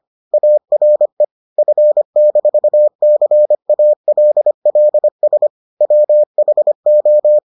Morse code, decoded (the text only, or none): ARE F-CALLS WHO